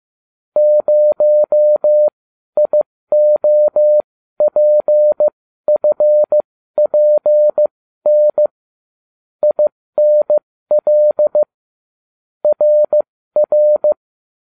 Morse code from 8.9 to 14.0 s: INL RR

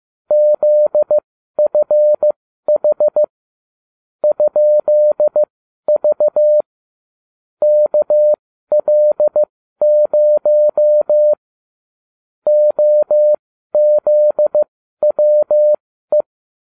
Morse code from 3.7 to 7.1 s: ?V

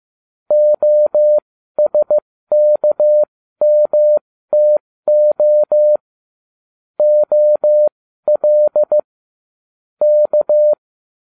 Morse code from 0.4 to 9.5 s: OSKMTO OL

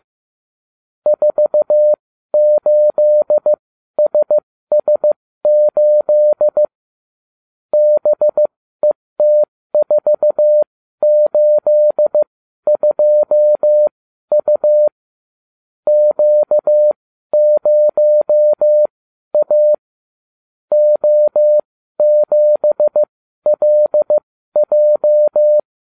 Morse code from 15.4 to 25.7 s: Q0A O7LJ

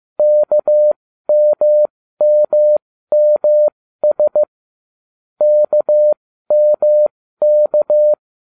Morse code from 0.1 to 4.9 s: KMMMS

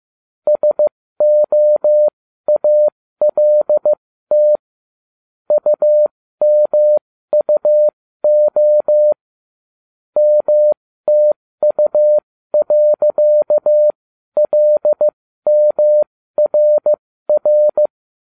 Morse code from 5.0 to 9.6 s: UMUO

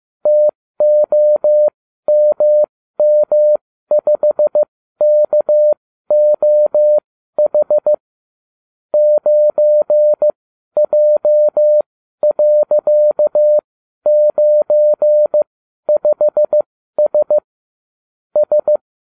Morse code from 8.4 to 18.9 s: 9J.95S S

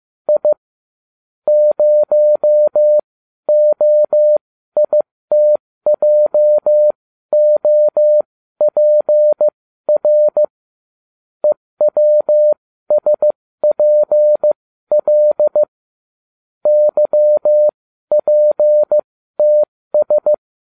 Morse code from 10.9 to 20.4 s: EWSPL YPTS

